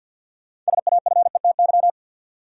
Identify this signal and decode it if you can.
Morse code, heard as a keyed tone, about 54 wpm, 700 Hz. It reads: SAFETY
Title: Sentences from Top 700 Words - New Word Drill - 50wpm